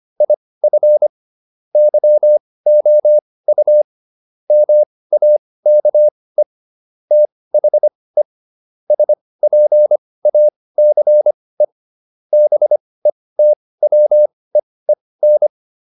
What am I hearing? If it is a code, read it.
Morse code, heard as a keyed tone, about 25 wpm, spaced as for 18 wpm, 600 Hz. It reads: IF YOU MAKE THE SPACE BETWEEN